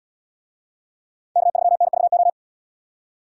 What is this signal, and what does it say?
Morse code, heard as a keyed tone, about 72 wpm, 700 Hz. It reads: K3NVO